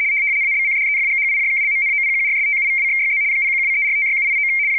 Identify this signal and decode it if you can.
RTTY, about 75 baud. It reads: ABCDEFGHIJKLMNOPQRSTUVWXYZ 0123456789
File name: w ry170 / rtty75.wav / rtty75